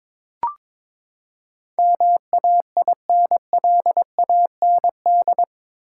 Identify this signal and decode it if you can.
Morse code, heard as a keyed tone, about 22 wpm, 700 Hz. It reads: MAINLAND
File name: practice / 22wpm / Eight Letter Words - 22wpm